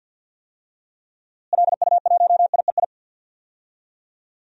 Morse code, heard as a keyed tone, about 53 wpm, 700 Hz. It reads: LU1IEI
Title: DXCC-100 Call Signs - 6 Characters - Repeated 2x - 50wpm